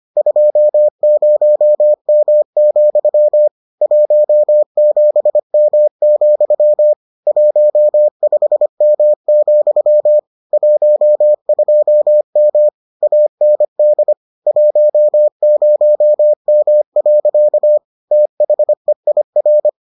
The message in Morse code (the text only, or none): 20M, 17M, 15M, 12M AND 10M. THEIR